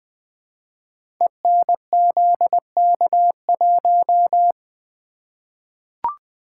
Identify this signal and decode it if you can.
Morse code, heard as a keyed tone, about 20 wpm, 700 Hz. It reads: ENZK1
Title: Code Groups - 5 Letter:Numbers - 20wpm